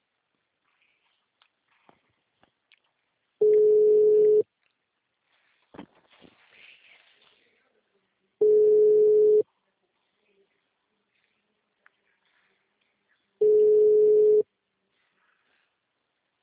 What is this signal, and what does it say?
Ringback tone (Europe, 425 Hz).